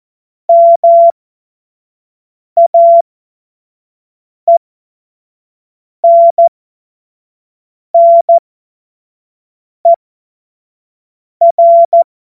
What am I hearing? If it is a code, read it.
Morse code, heard as a keyed tone, about 14 wpm, spaced as for 5 wpm, 680 Hz. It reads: MAENNER
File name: Morsekurs / Witze / Witz 25. Juli 2023 25 F70